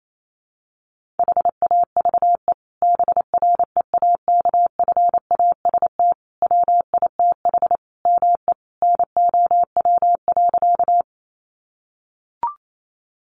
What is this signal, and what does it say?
Morse code, heard as a keyed tone, about 28 wpm, 700 Hz. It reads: HAVE BREAKFAST WITH ME NOW.
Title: Sentences from Top 1500 Words 28wpm